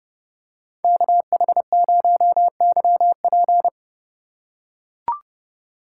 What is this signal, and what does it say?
Morse code, heard as a keyed tone, about 30 wpm, 700 Hz. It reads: KH0YP
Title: USA Call Signs - 5 Characters - 4x Character Spacing - 30wpm